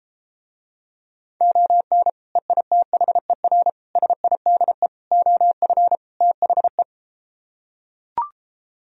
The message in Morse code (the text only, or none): ON EITHER SIDE OF THE